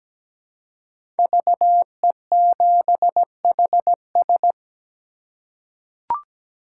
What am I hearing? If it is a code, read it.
Morse code, heard as a keyed tone, about 17 wpm, 700 Hz. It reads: VE7HS
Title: K1USN SST QSO Practice - 17wpm